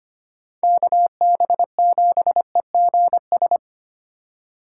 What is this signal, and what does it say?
Morse code, heard as a keyed tone, about 25 wpm, 700 Hz. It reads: KB7EGS
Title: USA Call Signs - 4x Character Spacing - 25wpm